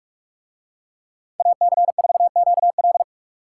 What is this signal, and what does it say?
Morse code, heard as a keyed tone, about 43 wpm, 700 Hz. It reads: AC4QL